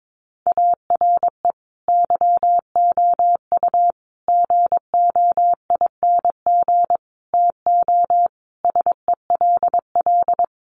Morse code, heard as a keyed tone, about 22 wpm, 700 Hz. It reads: ARE YOU GOING TO SELL
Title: Sentences from Top 1200 Words 22wpm